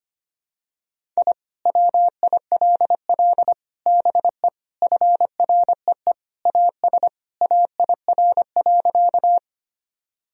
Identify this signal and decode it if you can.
Morse code, heard as a keyed tone, about 25 wpm, 700 Hz. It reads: I WILL BE FREE AS AIR.
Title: Sentences from Top 1000 Words - Review Part 1 - 25wpm